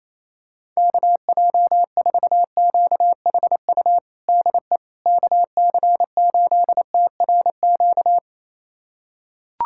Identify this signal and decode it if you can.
Morse code, heard as a keyed tone, about 28 wpm, 700 Hz. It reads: KJ4QHU DE KC8TRQ
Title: USA Callsign DE Callsign 28wpm